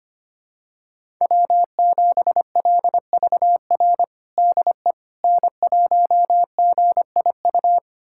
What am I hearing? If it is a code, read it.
Morse code, heard as a keyed tone, about 25 wpm, 700 Hz. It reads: W7LVR DE N1GIU
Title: USA Callsign DE Callsign - 5 Characters - 25wpm